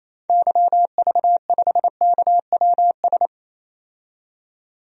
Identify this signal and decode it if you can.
Morse code, heard as a keyed tone, about 28 wpm, 700 Hz. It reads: YV5KWS
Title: DXCC-100 Call Signs - 6 Characters - 28wpm